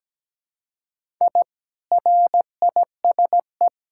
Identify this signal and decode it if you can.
Morse code, heard as a keyed tone, about 17 wpm, 700 Hz. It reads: I RISE
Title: Sentences from Top 1000 Words - Review Part 6 - 17wpm